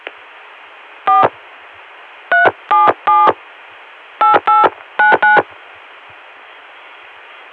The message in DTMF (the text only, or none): 4 377 8899